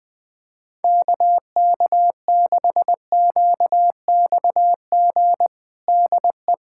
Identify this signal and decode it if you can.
Morse code, heard as a keyed tone, about 21 wpm, 700 Hz. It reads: KK6QXG DE
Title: USA Callsign DE Callsign - 6 Characters - 20wpm